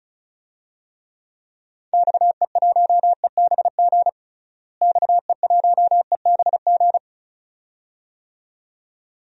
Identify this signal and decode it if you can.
Morse code, heard as a keyed tone, about 34 wpm, 700 Hz. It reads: XE1EBG XE1EBG
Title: DXCC-100 Call Signs - 6 Characters - Repeated 2x - 35wpm